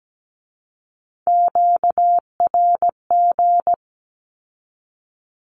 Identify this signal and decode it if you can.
Morse code, heard as a keyed tone, about 17 wpm, 700 Hz. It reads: QRG
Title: Q-Codes with Spoken Meaning - 17wpm